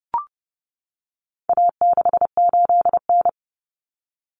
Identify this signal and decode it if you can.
Morse code, heard as a keyed tone, about 30 wpm, 700 Hz. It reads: A68N
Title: Common Cut Numbers - 4-Digit 30wpm